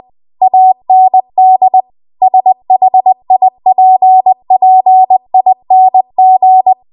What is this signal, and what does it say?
Morse code, heard as a keyed tone, about 20 wpm, 750 Hz. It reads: AND SHIPPING
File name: Archive / 20 WPM / 20WPM